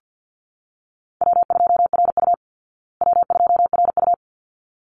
Morse code, heard as a keyed tone, about 50 wpm, 700 Hz. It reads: W2RU W2RU
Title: Easy Call Signs Repeated 2x - 50wpm